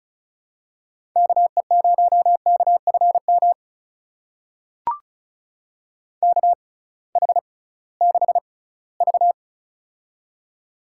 Morse code, heard as a keyed tone, about 36 wpm, spaced as for 20 wpm, 700 Hz. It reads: KE0KFM KH6V
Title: USA Call Signs - 6x Character Spacing - 35wpm